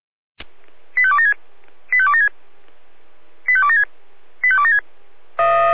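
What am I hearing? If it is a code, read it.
Five-tone selective call (ZVEI), heard as quick runs of five tones, tones of about 80 ms. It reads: 85276 85276 85276 85276
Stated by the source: ZVEI five-tone call 88133 received by SDR.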